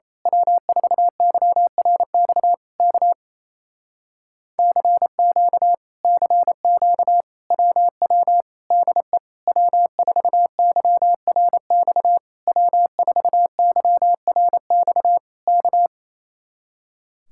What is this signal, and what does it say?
Morse code, heard as a keyed tone, about 26 wpm, 700 Hz. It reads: W4YRX K CQ CQ WW DE W4YRX W4YRX K
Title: Speed-Racing Common QSO Phrases Encore - 28wpm